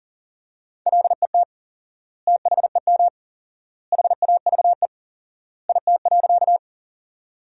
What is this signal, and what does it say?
Morse code, heard as a keyed tone, about 40 wpm, 700 Hz. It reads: LET THEM HAVE IT.
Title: 4x Word-Spacing - Sentences from Top 300 Words - 40wpm